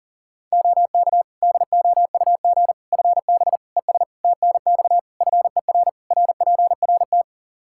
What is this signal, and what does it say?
Morse code, heard as a keyed tone, about 40 wpm, 700 Hz. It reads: OK DOUG FB ES TNX FER RPRT
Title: Practice QSO - Exchange 2 - Info - 40wpm